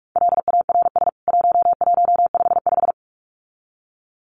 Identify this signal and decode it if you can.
Morse code, heard as a keyed tone, about 44 wpm, 700 Hz. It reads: LARS 1155
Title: CWT Exchange 45wpm